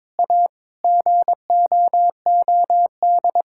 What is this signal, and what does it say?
Morse code, heard as a keyed tone, about 22 wpm, 700 Hz. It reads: A GOOD
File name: practice / 22wpm / Sentences with 8-Word Phrases - 22wpm